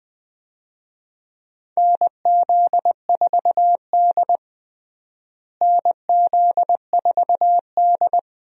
Morse code, heard as a keyed tone, about 21 wpm, 700 Hz. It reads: NZ4D NZ4D
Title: USA Call Signs - 4 Characters - Repeated 2x - 20wpm